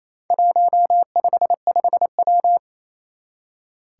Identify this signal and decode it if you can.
Morse code, heard as a keyed tone, about 27 wpm, 700 Hz. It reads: 155W
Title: Four Challenging Letter-Number Combinations 28wpm